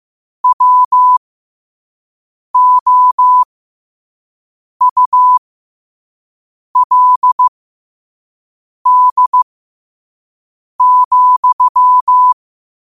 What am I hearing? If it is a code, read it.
Morse code, heard as a keyed tone, about 15 wpm, spaced as for 5 wpm, 1000 Hz. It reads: WOULD,
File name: qotdfiles / 5wpm